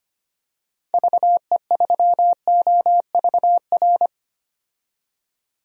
Morse code, heard as a keyed tone, about 25 wpm, 700 Hz. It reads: VE3OVR
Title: DXCC-100 Call Signs - 6 Characters - 25wpm